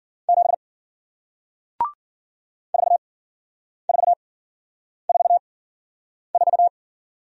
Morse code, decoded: 7 4444

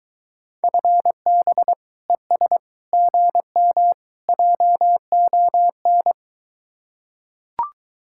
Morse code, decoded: FB ES GM JON

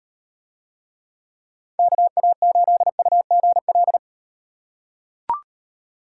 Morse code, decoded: KA8UGL

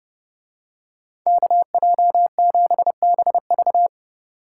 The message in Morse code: KJ7BV

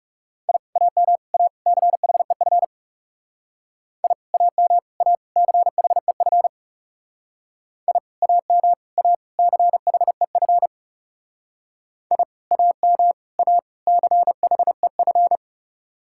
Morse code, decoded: I AM A CHEF I AM A CHEF I AM A CHEF I AM A CHEF

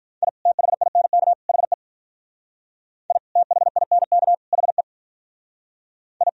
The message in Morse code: I THINK HE I THINK HE I

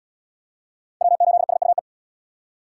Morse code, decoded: W8SUE

